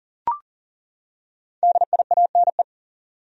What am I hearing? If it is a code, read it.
Morse code, heard as a keyed tone, about 40 wpm, 700 Hz. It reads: DIANE
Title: USA Common Names - 40wpm